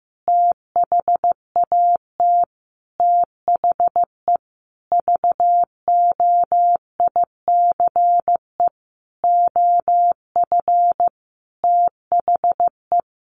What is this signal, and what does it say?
Morse code, heard as a keyed tone, about 15 wpm, 700 Hz. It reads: THAT THE VOICE OF THE
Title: Sentences from Top 1500 Words 15wpm